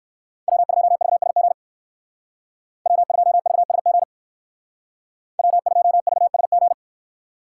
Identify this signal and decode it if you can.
Morse code, heard as a keyed tone, about 59 wpm, 700 Hz. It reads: W2VSG W2VSG W2VSG